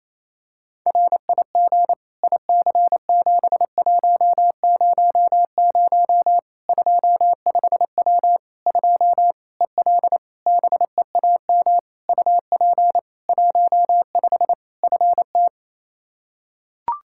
Morse code, heard as a keyed tone, about 28 wpm, 700 Hz. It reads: RIG IC7100 25W 2 EL BEAM UP 15 FT
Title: K1USN SST QSO Practice 28wpm